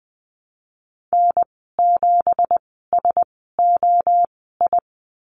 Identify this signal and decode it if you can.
Morse code, heard as a keyed tone, about 19 wpm, spaced as for 14 wpm, 700 Hz. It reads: N7SOI
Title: 193 Lesson 31g - Letter v - Call Signs 20wpm